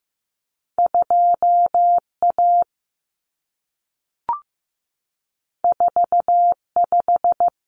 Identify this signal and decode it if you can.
Morse code, heard as a keyed tone, about 15 wpm, 700 Hz. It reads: 2A 45
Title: Common Cut Numbers - 2-Digit 15wpm